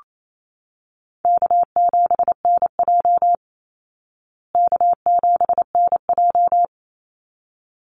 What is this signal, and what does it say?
Morse code, heard as a keyed tone, about 28 wpm, 700 Hz. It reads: K7NJ K7NJ
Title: Call Signs Repeated 3x - 28wpm